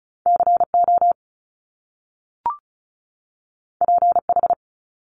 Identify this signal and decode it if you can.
Morse code, heard as a keyed tone, about 36 wpm, 700 Hz. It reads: C O P H
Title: Top Two Letter Combinations in Words 35wpm